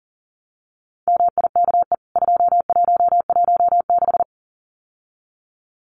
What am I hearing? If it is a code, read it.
Morse code, heard as a keyed tone, about 40 wpm, 700 Hz. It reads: MIKE 2116
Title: CWT Exchange 40wpm